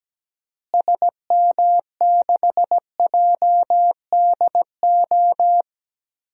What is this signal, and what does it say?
Morse code, heard as a keyed tone, about 17 wpm, 700 Hz. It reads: SM6JDO
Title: DXCC-100 Call Signs - 6 Characters - Repeated 3x - 17wpm